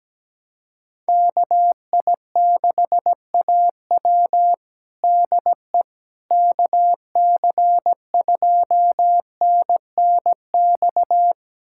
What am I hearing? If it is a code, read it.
Morse code, heard as a keyed tone, about 17 wpm, 700 Hz. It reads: KI6AW DE KC2NNX
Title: USA Callsign DE Callsign 17wpm